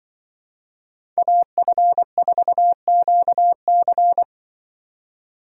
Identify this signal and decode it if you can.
Morse code, heard as a keyed tone, about 23 wpm, 700 Hz. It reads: AF4QC
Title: Speed Racing - USA 2x2 Call Signs 20wpm